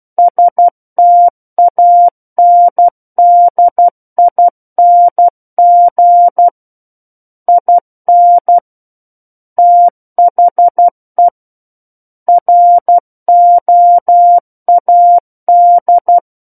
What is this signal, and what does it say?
Morse code, heard as a keyed tone, about 12 wpm, 700 Hz. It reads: STANDING IN THE ROAD
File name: Book One / 03-Chapter 3 - 12 WPM